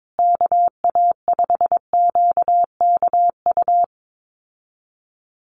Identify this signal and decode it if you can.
Morse code, heard as a keyed tone, about 22 wpm, 700 Hz. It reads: KA5QKU
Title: USA Call Signs Repeated 2x 22wpm